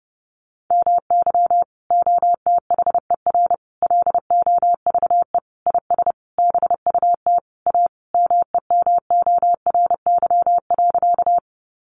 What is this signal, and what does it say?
Morse code, heard as a keyed tone, about 30 wpm, 700 Hz. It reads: MY OTHER LOVE IS BUT A MEMORY.
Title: Sentences from Top 1300 Words 30wpm